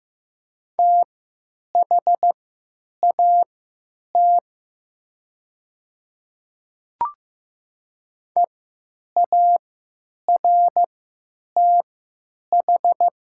Morse code, heard as a keyed tone, about 15 wpm, spaced as for 9 wpm, 700 Hz. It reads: THAT EARTH
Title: 3x Character Spacing - Top 500 Words - 15wpm